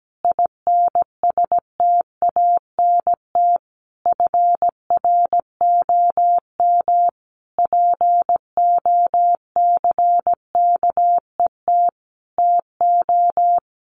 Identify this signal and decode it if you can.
Morse code, heard as a keyed tone, about 17 wpm, 700 Hz. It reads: INSTANT FROM POCKET TO
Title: A Sound Of Thunder 17wpm